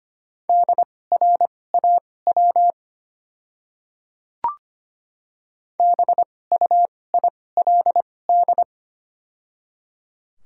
Morse code, heard as a keyed tone, about 25 wpm, spaced as for 18 wpm, 690 Hz. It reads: DRAW BUILD
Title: 2x Character Spacing - Top 300 Words - 25wpm